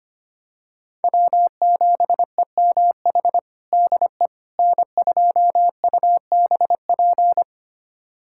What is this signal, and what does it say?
Morse code, heard as a keyed tone, about 25 wpm, 700 Hz. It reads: W7EMH DE N2UBP
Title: USA Callsign DE Callsign - 5 Characters - 25wpm